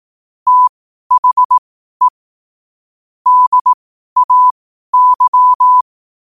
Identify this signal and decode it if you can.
Morse code, heard as a keyed tone, about 18 wpm, spaced as for 12 wpm, 1000 Hz. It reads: THE DAY
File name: qotdfiles / 12wpm